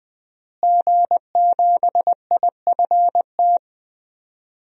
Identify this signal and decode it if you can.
Morse code, heard as a keyed tone, about 21 wpm, 700 Hz. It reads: G7IFT
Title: DXCC-100 Call Signs - 5 Characters - Repeated 3x - 20wpm